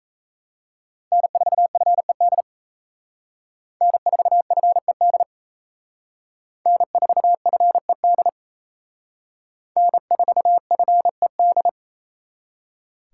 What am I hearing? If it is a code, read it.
Morse code, heard as a keyed tone, about 35 wpm, 700 Hz. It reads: N4FED N4FED N4FED N4FED
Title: Speed Racing - USA Call Signs 28wpm